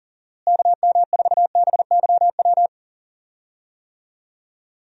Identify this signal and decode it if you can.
Morse code, heard as a keyed tone, about 39 wpm, 700 Hz. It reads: KM4BYW
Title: USA Call Signs - 6 Characters - Repeated 3x - 40wpm